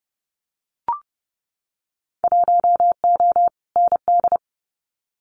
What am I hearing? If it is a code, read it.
Morse code, heard as a keyed tone, about 30 wpm, 700 Hz. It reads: 1O ND
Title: Winter Field Day Exchanges 30wpm